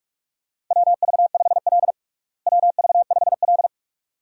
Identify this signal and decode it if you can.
Morse code, heard as a keyed tone, about 46 wpm, 700 Hz. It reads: WV5L WV5L
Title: USA Call Signs - 4 Characters - Repeated 2x - 45wpm